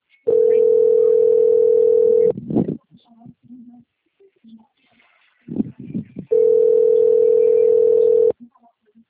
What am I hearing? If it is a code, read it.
Ringback tone (North America), 440 + 480 Hz, with speech.
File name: out- - - - - .24238